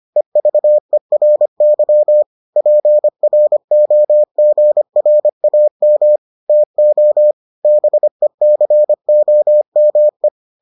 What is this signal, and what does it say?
Morse code, heard as a keyed tone, about 25 wpm, 600 Hz. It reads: EVERY PROGRAM TO BECOME